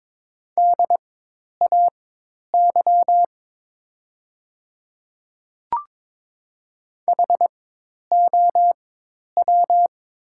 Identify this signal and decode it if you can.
Morse code, heard as a keyed tone, about 22 wpm, spaced as for 10 wpm, 700 Hz. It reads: DAY HOW
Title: 4x Character Spacing - Top 300 Words - 22wpm